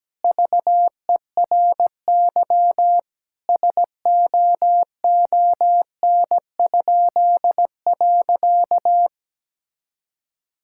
Morse code, heard as a keyed tone, about 17 wpm, 700 Hz. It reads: VERY SOON?.
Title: Pangram Challenge v2 - 17wpm